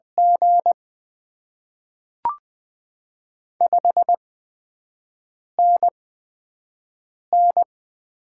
Morse code, heard as a keyed tone, about 20 wpm, spaced as for 5 wpm, 700 Hz. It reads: G 5NN
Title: Top 20 QSO Elements - 8x Character Spacing - 20wpm